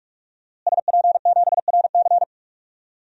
Morse code, heard as a keyed tone, about 46 wpm, 700 Hz. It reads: SP7RC
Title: DXCC-100 Call Signs - 5 Characters - Repeated 2x - 45wpm